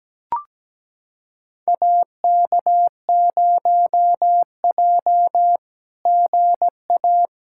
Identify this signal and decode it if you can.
Morse code, heard as a keyed tone, about 17 wpm, 700 Hz. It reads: AK0J GA
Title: SOTA QSO Practice 17wpm